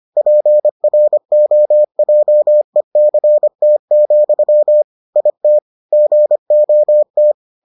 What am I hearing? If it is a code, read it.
Morse code, heard as a keyed tone, about 25 wpm, 600 Hz. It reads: PROJECT, IT GOT